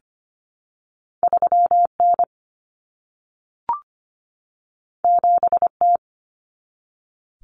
Morse code, heard as a keyed tone, about 25 wpm, 700 Hz. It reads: 3 N 7 T
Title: Common Cut Numbers - 2-Digit 25wpm